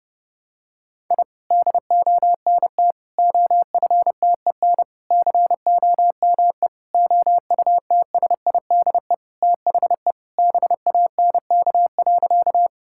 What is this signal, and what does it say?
Morse code, heard as a keyed tone, about 30 wpm, 700 Hz. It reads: I DONT OFTEN COME OUTSIDE THE BANK.
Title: Sentences from Top 1000 Words - Review Part 6 - 30wpm